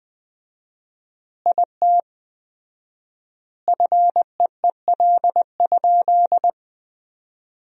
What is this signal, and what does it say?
Morse code, heard as a keyed tone, about 20 wpm, 700 Hz. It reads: IT FEEL?
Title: 4x Word-Spacing - Sentences from Top 400 Words - 20wpm